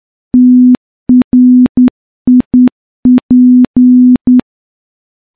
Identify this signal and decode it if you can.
Morse code, heard as a keyed tone, about 10 wpm, 250 Hz. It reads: TRIP
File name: morse-training-10WPM